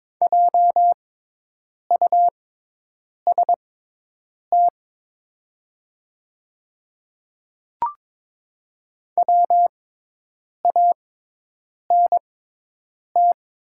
Morse code, heard as a keyed tone, about 23 wpm, spaced as for 8 wpm, 700 Hz. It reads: JUST WANT
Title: 6x Character Spacing - Top 200 Words - 22wpm